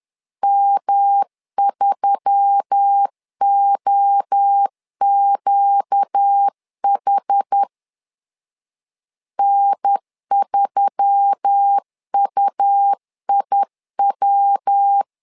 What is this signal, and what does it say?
Morse code, heard as a keyed tone, about 10 wpm, 800 Hz. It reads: M3OQH N3UIW
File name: LEZIONI / 4 Q1S3 10wpm 2427